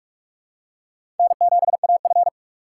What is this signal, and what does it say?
Morse code, heard as a keyed tone, about 47 wpm, 700 Hz. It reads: N7AF
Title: USA Call Signs - 4 Characters - 2x Character Spacing - 45wpm